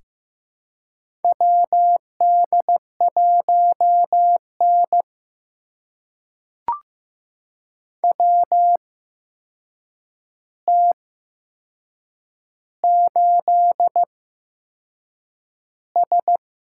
Morse code, heard as a keyed tone, about 15 wpm, spaced as for 6 wpm, 700 Hz. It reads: WD1N WT8S
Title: USA Call Signs - 4 Characters - 8x Character Spacing - 15wpm